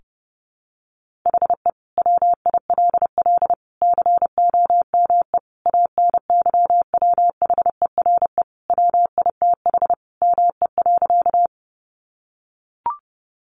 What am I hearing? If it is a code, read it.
Morse code, heard as a keyed tone, about 30 wpm, 700 Hz. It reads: HE WILL COME ANYWHERE WITH ME.
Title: Sentences from Top 1400 Words 30wpm